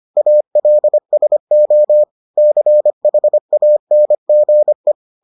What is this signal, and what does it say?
Morse code, heard as a keyed tone, about 25 wpm, 600 Hz. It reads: ALSO CHANGE